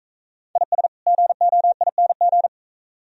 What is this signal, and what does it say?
Morse code, heard as a keyed tone, about 42 wpm, 690 Hz. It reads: IS GOING